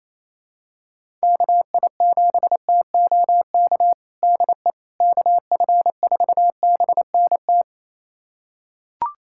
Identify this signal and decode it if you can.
Morse code, heard as a keyed tone, about 28 wpm, 700 Hz. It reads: KI7TOK DE KF4BNT
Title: USA Callsign DE Callsign - 6 Characters - 28wpm